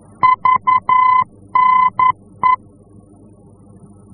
Morse code, heard as a keyed tone, about 11 wpm, 1020 Hz. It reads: VNE
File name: france / mx / VNE-F-VOR